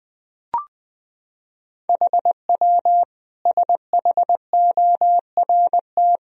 Morse code, heard as a keyed tone, about 20 wpm, 700 Hz. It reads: HW SHORT
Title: Sentences from Top 400 Words and Abbreviations 20wpm